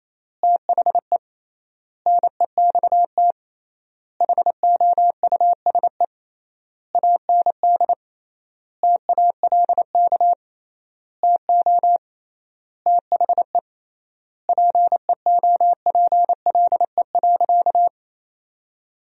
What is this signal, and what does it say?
Morse code, heard as a keyed tone, about 28 wpm, 700 Hz. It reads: THE NEXT HOUSE AND TALK TO THE PEOPLE.
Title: 3x Word-Spacing - Sentences from Top 400 Words - 28wpm